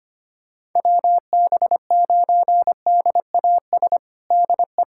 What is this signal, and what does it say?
Morse code, heard as a keyed tone, about 25 wpm, 700 Hz. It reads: WB9DAS DE
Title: USA Callsign DE Callsign - 6 Characters - 25wpm